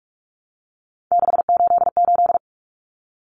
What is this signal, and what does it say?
Morse code, heard as a keyed tone, about 44 wpm, 700 Hz. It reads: 688
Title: Numbers - 3-Digit 45wpm